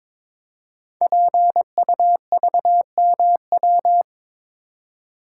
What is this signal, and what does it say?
Morse code, heard as a keyed tone, about 22 wpm, 700 Hz. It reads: PUVMW
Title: Code Groups - 5 Letters - 22wpm